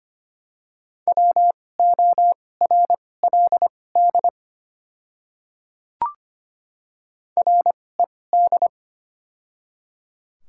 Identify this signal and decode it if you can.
Morse code, heard as a keyed tone, about 25 wpm, spaced as for 18 wpm, 700 Hz. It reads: WORLD RED
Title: 2x Character Spacing - Top 500 Words - 25wpm